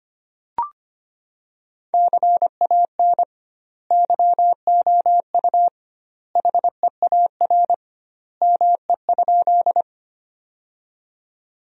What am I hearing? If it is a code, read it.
Morse code, heard as a keyed tone, about 25 wpm, 700 Hz. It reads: CAN YOU HEAR ME?
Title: 2x Word-Spacing - Sentences from Top 300 Words - 25wpm